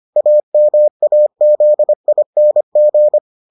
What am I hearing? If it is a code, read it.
Morse code, heard as a keyed tone, about 25 wpm, 600 Hz. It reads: AMAZING